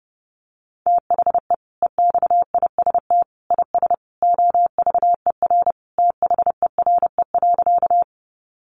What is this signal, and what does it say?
Morse code, heard as a keyed tone, about 30 wpm, 700 Hz. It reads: THE EXIST IS OVER THERE.
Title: Sentences from Top 1100 Words 30wpm